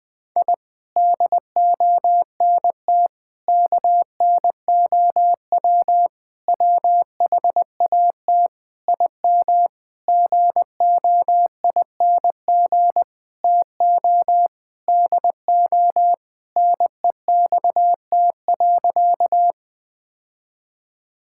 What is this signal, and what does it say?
Morse code, heard as a keyed tone, about 20 wpm, 700 Hz. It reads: I DONT KNOW WHAT IM GOING TO DO NEXT.
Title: Sentences with 8-Word Phrases - 20wpm